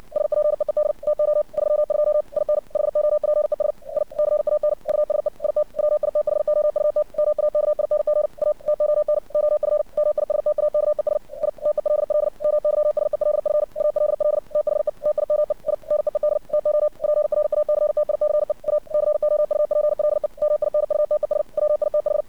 Morse code, heard as a keyed tone, about 56 wpm, 600 Hz. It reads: SPEED TO 32 IN HOPES I CAN USE IT WITHOUT MAKING A TON OF MISTAKES I TELL MYSELF ALL THE TIME I NEED TO PRACTICE A COUPLE MINUTES WITH
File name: archived / CRSnet